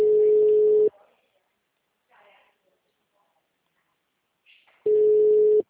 Ringback tone (Europe, 425 Hz).